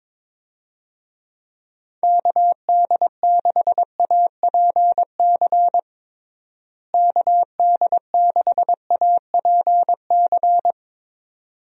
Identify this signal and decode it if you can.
Morse code, heard as a keyed tone, about 22 wpm, 700 Hz. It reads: KD6APC KD6APC
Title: USA Call Signs - 6 Characters - Repeated 2x - 22wpm